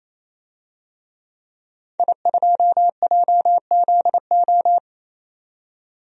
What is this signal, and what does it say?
Morse code, heard as a keyed tone, about 27 wpm, 700 Hz. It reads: I2JZO
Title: DXCC-100 Call Signs - 5 Characters - 28wpm